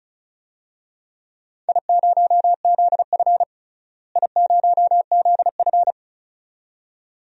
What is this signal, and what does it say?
Morse code, heard as a keyed tone, about 36 wpm, 700 Hz. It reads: I0ZF I0ZF
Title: DXCC-100 Call Signs - 4 Characters - Repeated 2x - 35wpm